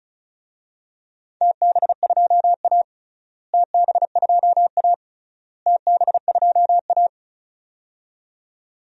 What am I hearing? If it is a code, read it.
Morse code, heard as a keyed tone, about 34 wpm, 700 Hz. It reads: TB2A TB2A TB2A
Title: DXCC-100 Call Signs - 4 Characters - Repeated 3x - 35wpm